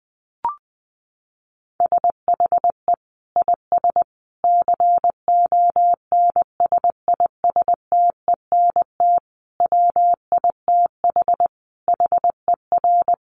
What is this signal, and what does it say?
Morse code, heard as a keyed tone, about 20 wpm, 700 Hz. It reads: SHE IS CONSISTENT WITH HER
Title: Sentences from Top 1500 Words 20wpm